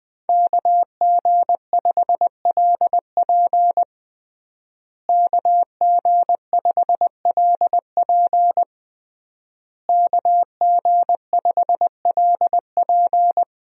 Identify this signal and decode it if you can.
Morse code, heard as a keyed tone, about 20 wpm, 700 Hz. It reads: KG5LP KG5LP KG5LP
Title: USA Call Signs - 5 Characters - Repeated 3x - 20wpm